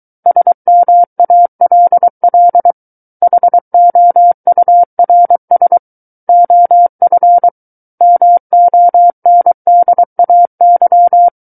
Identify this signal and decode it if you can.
Morse code, heard as a keyed tone, about 23 wpm, 700 Hz. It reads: SMALL HOURS OF MONDAY
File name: Book One / 14-Chapter 14 - 23 WPM